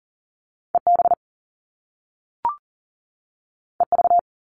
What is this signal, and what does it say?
Morse code, heard as a keyed tone, about 39 wpm, 700 Hz. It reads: EB EV
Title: Top Two Letter Combinations in Words 40wpm